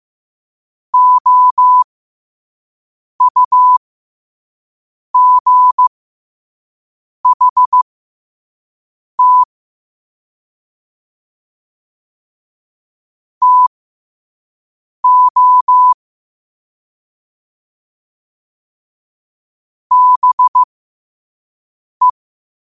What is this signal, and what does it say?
Morse code, heard as a keyed tone, about 15 wpm, spaced as for 5 wpm, 1000 Hz. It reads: OUGHT TO BE
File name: qotdfiles / 5wpm